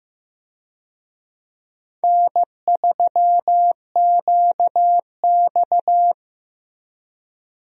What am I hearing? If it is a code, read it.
Morse code, heard as a keyed tone, about 16 wpm, 700 Hz. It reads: N3QX
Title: USA Call Signs - 4 Characters - Repeated 3x - 15wpm